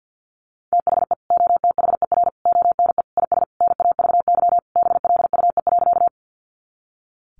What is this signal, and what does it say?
Morse code, heard as a keyed tone, about 50 wpm, 700 Hz. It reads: THE OTHER ONE IS NAVY BLUE.
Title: Additional Easy Phrases 50wpm